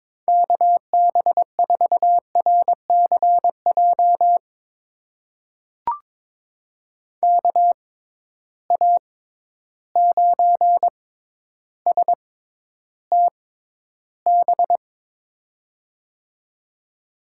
Morse code, heard as a keyed tone, about 22 wpm, spaced as for 11 wpm, 700 Hz. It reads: KB4RCJ KA9STB
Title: USA Call Signs - 6 Characters - 6x Character Spacing - 22wpm